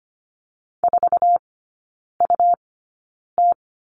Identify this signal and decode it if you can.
Morse code, heard as a keyed tone, about 25 wpm, spaced as for 9 wpm, 700 Hz. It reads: 4UT
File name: Lesson 17e - Letter u - Three Characters 25wpm